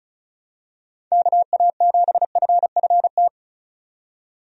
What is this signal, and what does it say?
Morse code, heard as a keyed tone, about 36 wpm, 700 Hz. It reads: KA7FFT
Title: USA Call Signs - 6 Characters - 35wpm